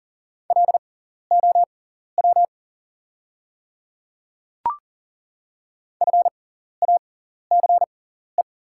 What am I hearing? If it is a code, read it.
Morse code, heard as a keyed tone, about 39 wpm, spaced as for 14 wpm, 690 Hz. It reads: LOW FACE
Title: 6x Character Spacing - Top 500 Words - 40wpm